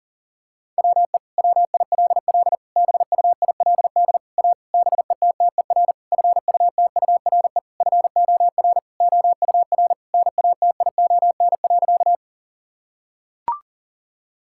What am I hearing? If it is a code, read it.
Morse code, heard as a keyed tone, about 40 wpm, 700 Hz. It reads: WE WILL BUILD A BETTER FUTURE FOR OUR NATION.
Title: Sentences from Top 1000 Words - Review Part 6 - 40wpm